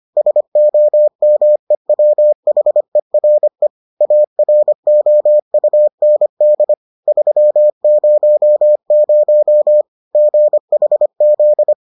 Morse code, heard as a keyed tone, about 25 wpm, 600 Hz. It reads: SOMEWHERE AROUND 300 GHZ